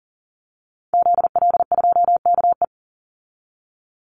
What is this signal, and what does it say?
Morse code, heard as a keyed tone, about 40 wpm, 700 Hz. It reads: ZL2KE
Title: Call Signs - 40wpm